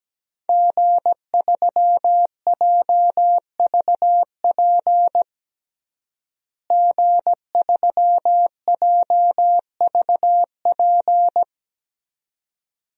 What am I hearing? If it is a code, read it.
Morse code, heard as a keyed tone, about 17 wpm, 700 Hz. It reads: G3JVP G3JVP
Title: DXCC-100 Call Signs - 5 Characters - Repeated 2x - 17wpm